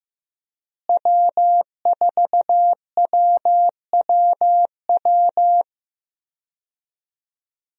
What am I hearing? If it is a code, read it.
Morse code, heard as a keyed tone, about 15 wpm, 700 Hz. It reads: W4WWW
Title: USA Call Signs - 5 Characters - Repeated 2x - 15wpm